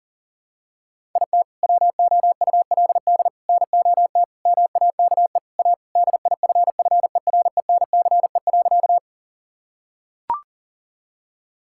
Morse code, heard as a keyed tone, about 37 wpm, 700 Hz. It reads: IT WOULD NOT MAKE A DIFFERENCE.
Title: Sentences from Top 1000 Words - Review Part 5 - 40wpm